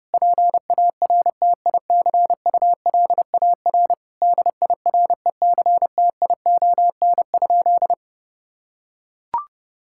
Morse code, heard as a keyed tone, about 30 wpm, 700 Hz. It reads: PARTICULAR DIRECTION?
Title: Sentences from Top 1000 Words - Review Part 5 - 30wpm